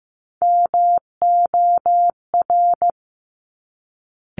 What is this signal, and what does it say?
Morse code, heard as a keyed tone, about 15 wpm, 700 Hz. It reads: MOR
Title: Top Three Letter Combinations in Words 15wpm